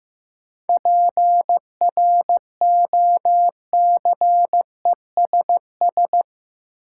Morse code, heard as a keyed tone, about 15 wpm, 700 Hz. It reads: PROCESS
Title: Sentences from Top 1000 Words - Review Part 3 - New Word Drill - 15wpm